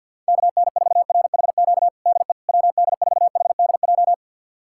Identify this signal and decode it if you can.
Morse code, heard as a keyed tone, about 52 wpm, 700 Hz. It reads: KN4RHQ DE WD4HDJ
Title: USA Callsign DE Callsign - 6 Characters - 50wpm